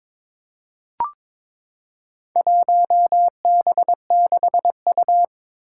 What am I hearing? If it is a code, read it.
Morse code, heard as a keyed tone, about 21 wpm, 700 Hz. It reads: 1B6U
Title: Four Challenging Letter-Number Combinations 22wpm